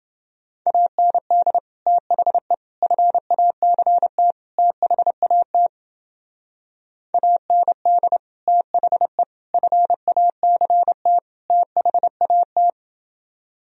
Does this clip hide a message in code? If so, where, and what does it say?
Yes: Morse code from 0.7 to 12.7 s, heard as a keyed tone, about 29 wpm, 700 Hz. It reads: AND THE FACT THAT AND THE FACT THAT